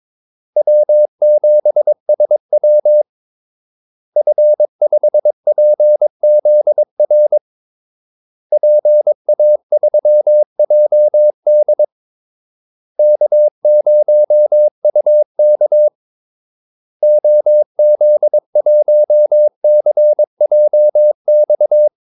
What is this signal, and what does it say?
Morse code, heard as a keyed tone, about 22 wpm, 600 Hz. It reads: W7SW F5PZR PA3JD K0UK OZ1CJX
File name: Callsigns-22wpm